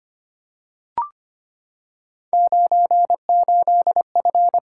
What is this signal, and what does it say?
Morse code, heard as a keyed tone, about 25 wpm, 700 Hz. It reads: 98F
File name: Three Letter-Number Combinations - 25wpm